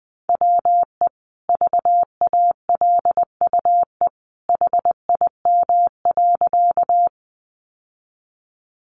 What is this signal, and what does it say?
Morse code, heard as a keyed tone, about 20 wpm, 700 Hz. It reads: WE VALUE HIM.
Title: Sentences from Top 600 Words 20wpm